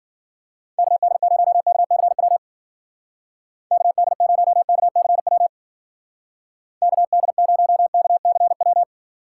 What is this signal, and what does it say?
Morse code, heard as a keyed tone, about 52 wpm, 700 Hz. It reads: KD0KCW KD0KCW KD0KCW